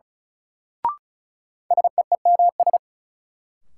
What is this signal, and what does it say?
Morse code, heard as a keyed tone, about 36 wpm, 700 Hz. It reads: SEEMS
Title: Sets of 3 Words - New Word Drill - 35wpm